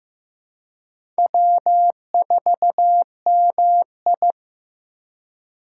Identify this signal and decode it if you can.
Morse code, heard as a keyed tone, about 15 wpm, 700 Hz. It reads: W4MI
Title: USA Call Signs - 2x Character Spacing - 15wpm